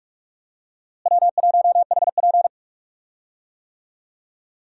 Morse code, heard as a keyed tone, about 41 wpm, 700 Hz. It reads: W1HP
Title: USA Call Signs - 4 Characters - Repeated 3x - 45wpm